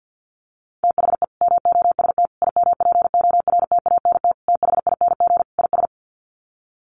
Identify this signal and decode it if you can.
Morse code, heard as a keyed tone, about 50 wpm, 700 Hz. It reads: THE MOST IMPORTANT THING IS
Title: Sets of 5 Words Encore 50wpm